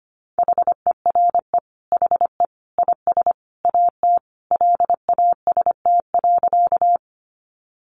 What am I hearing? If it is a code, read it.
Morse code, heard as a keyed tone, about 25 wpm, 700 Hz. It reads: HERE HE IS AT LAST.
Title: Sentences from Top 300 Words 25wpm